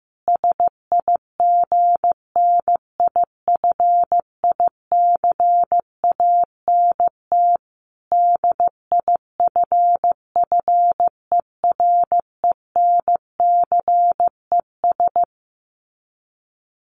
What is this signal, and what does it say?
Morse code, heard as a keyed tone, about 15 wpm, 700 Hz. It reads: SIGNIFICANT DIFFERENCES
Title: Sets of 5 Words Encore2 15wpm